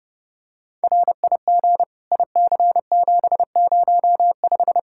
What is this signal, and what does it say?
Morse code, heard as a keyed tone, about 30 wpm, 700 Hz. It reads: RIG IC705
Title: Practice QSO - Exchange 2 - Info - 30wpm